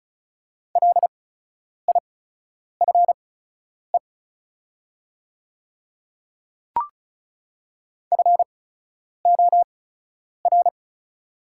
Morse code, heard as a keyed tone, about 36 wpm, spaced as for 10 wpm, 690 Hz. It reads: LIFE FOR
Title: 8x Character Spacing - Top 500 Words - 35wpm